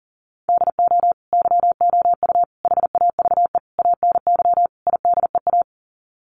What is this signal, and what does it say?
Morse code, heard as a keyed tone, about 40 wpm, 700 Hz. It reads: DO YOU HAVE ANY IDEA
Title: Sets of 5 Words Encore 40wpm